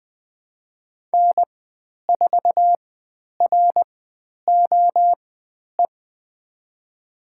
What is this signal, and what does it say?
Morse code, heard as a keyed tone, about 20 wpm, spaced as for 10 wpm, 700 Hz. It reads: N4ROE
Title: 095 Lesson 16g - Letter c - Call Signs 20wpm